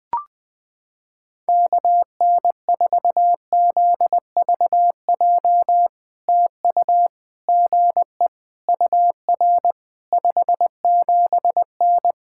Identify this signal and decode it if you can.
Morse code, heard as a keyed tone, about 21 wpm, 700 Hz. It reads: KN4ZVJ TU GE UR 57N
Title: POTA QSO Practice - 20wpm